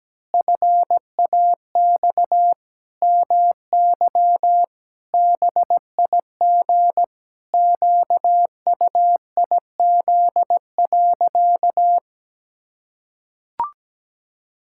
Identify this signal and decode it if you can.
Morse code, heard as a keyed tone, about 17 wpm, 700 Hz. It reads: FAX MY BIG QUIZ.
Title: Pangram Challenge v2 - 17wpm